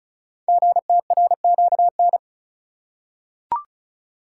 Morse code, heard as a keyed tone, about 36 wpm, 700 Hz. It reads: GTRQN
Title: Code Groups - 5 Letters - 35wpm